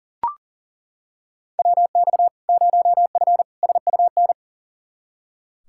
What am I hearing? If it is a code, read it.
Morse code, heard as a keyed tone, about 37 wpm, 700 Hz. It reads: WX 0F SUN
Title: Practice QSO - Exchange 2 - Info - 40wpm